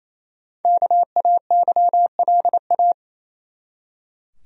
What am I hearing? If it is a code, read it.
Morse code, heard as a keyed tone, about 28 wpm, 700 Hz. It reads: KAYLA
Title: USA Common Names - Max 5 Letters - 28wpm